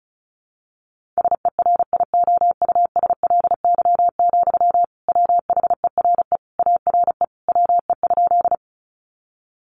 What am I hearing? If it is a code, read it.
Morse code, heard as a keyed tone, about 34 wpm, 700 Hz. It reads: SERIOUSLY, WHERE ARE WE?
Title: Sentences from Top 1200 Words 35wpm